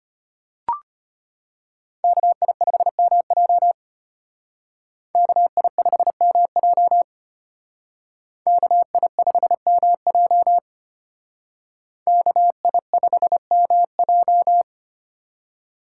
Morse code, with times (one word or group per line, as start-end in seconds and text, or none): 2.0-3.7 KI5MJ
5.2-7.0 KI5MJ
8.5-10.6 KI5MJ
12.1-14.6 KI5MJ